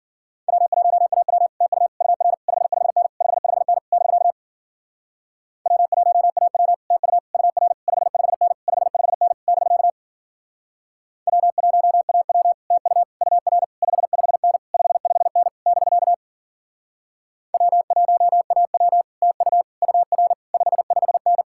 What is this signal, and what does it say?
Morse code, heard as a keyed tone, about 52 wpm, 700 Hz. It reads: W1AW TU UR 55N 55N <BK> W1AW TU UR 55N 55N <BK> W1AW TU UR 55N 55N <BK> W1AW TU UR 55N